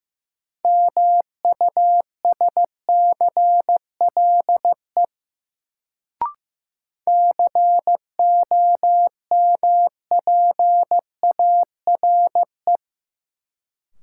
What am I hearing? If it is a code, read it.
Morse code, heard as a keyed tone, about 15 wpm, 700 Hz. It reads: MUSCLE COMPARE
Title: Sentences from Top 1300 Words - New Word Drill Alt - 15wpm